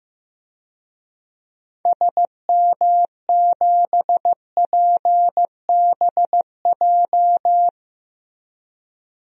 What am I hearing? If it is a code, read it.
Morse code, heard as a keyed tone, about 15 wpm, 700 Hz. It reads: SM7PBJ
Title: DXCC-100 Call Signs - 6 Characters - Repeated 2x - 15wpm